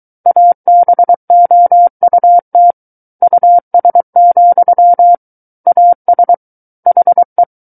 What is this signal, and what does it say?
Morse code, heard as a keyed tone, about 23 wpm, 700 Hz. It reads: ABOUT US, AS HE
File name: Book One / 14-Chapter 14 - 23 WPM